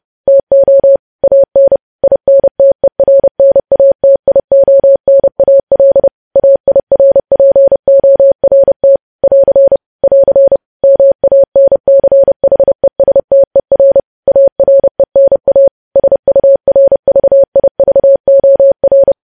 Morse code, heard as a keyed tone, about 30 wpm, 550 Hz. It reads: TO AN INTERNATIONAL AIRPORT <AR> <AR> MANCHESTER ARENA SURVIVOR
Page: news headlines in morse code at 30 wpm